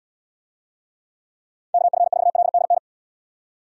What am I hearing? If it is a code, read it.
Morse code, heard as a keyed tone, about 74 wpm, 700 Hz. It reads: KF4LDN